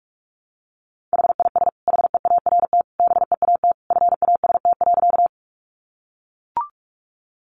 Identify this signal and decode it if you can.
Morse code, heard as a keyed tone, about 44 wpm, 700 Hz. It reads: HIS HEART BEAT FAST.
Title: Sentences from Top 1300 Words 45wpm